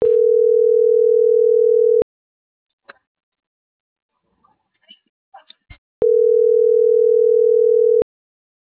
Ringback tone (North America), 440 + 480 Hz.